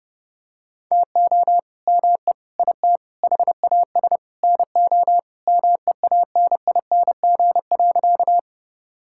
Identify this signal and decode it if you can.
Morse code, heard as a keyed tone, about 32 wpm, 700 Hz. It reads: TO ME IT HAS NO MEANING.
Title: Sentences from Top 1000 Words - Review Part 6 - 30wpm